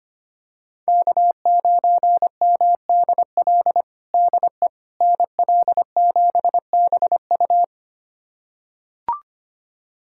Morse code, heard as a keyed tone, about 25 wpm, 700 Hz. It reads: K9MDL DE NL7BU
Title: USA Callsign DE Callsign - 5 Characters - 25wpm